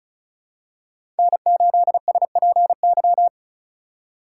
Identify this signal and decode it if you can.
Morse code, heard as a keyed tone, about 36 wpm, 700 Hz. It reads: N8SPY
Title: USA Call Signs - 5 Characters - 2x Character Spacing - 35wpm